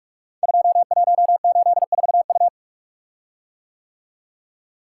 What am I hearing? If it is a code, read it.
Morse code, heard as a keyed tone, about 44 wpm, 700 Hz. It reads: 2184U
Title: Code Groups - 5 Letter:Numbers - 45wpm